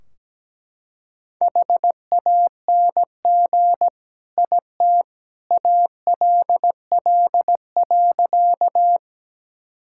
Morse code, heard as a keyed tone, about 17 wpm, 700 Hz. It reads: HANG IT ALL.
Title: Sentences from Top 1600 Words 17wpm